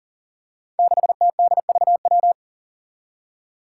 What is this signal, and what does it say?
Morse code, heard as a keyed tone, about 40 wpm, 700 Hz. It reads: 6TDVW
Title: Five Challenging Letter-Number Combinations 40wpm